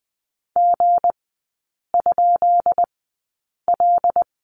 Morse code, heard as a keyed tone, about 20 wpm, spaced as for 8 wpm, 700 Hz. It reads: G?L
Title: 218 Lesson 35e - Number 0 - Three Characters 20wpm